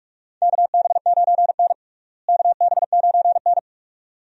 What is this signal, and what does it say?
Morse code, heard as a keyed tone, about 44 wpm, 700 Hz. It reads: KB9N KB9N
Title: USA Call Signs - 4 Characters - Repeated 2x - 45wpm